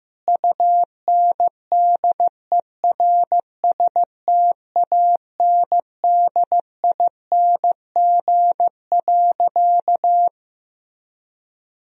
Morse code, heard as a keyed tone, about 15 wpm, 700 Hz. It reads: UNDERSTANDING.
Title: Sentences from Top 1000 Words - Review Part 3 - 15wpm